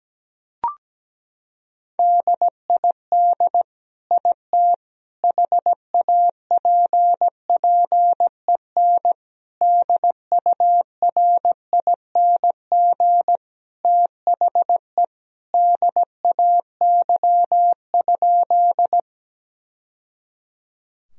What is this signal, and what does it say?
Morse code, heard as a keyed tone, about 17 wpm, 700 Hz. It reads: DID IT HAPPEN DURING THE DAY?
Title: Sentences from Top 1000 Words - Review Part 2 - 17wpm